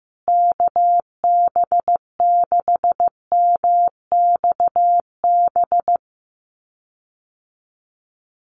Morse code, heard as a keyed tone, about 15 wpm, 700 Hz. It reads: KB6MXB
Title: USA Call Signs Repeated 2x 15wpm